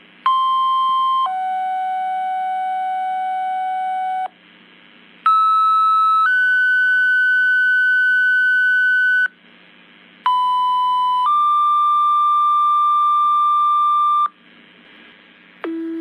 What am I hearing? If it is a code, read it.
Two-tone page, heard as a one-second tone, then a three-second tone, about 1050 then 750 Hz, about 1300 then 1500 Hz, about 1000 then 1190 Hz, with noise. It is heard from 0.3 to 14.3 s.
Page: motorola pager tone